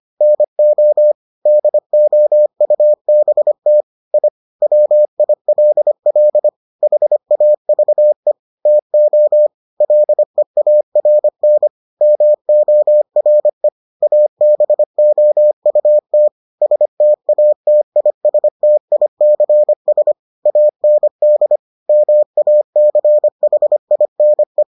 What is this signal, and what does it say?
Morse code, heard as a keyed tone, about 25 wpm, 600 Hz. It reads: NO DOUBT I WILL HAVE TO LEARN MORE ABOUT STATISTICS AND MACHINE